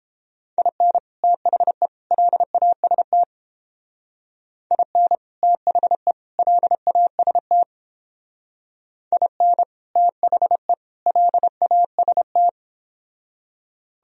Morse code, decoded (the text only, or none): IN THE LAST IN THE LAST IN THE LAST